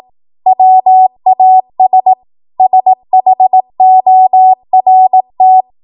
Morse code, heard as a keyed tone, about 18 wpm, 750 Hz. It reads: WAS SHORT
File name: Archive / 18 WPM / 18WPM